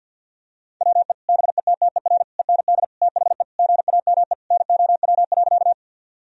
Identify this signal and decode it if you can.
Morse code, heard as a keyed tone, about 50 wpm, 700 Hz. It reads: WE BETTER END THE GAME NOW.
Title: Sentences from Top 1000 Words - Review Part 3 - 50wpm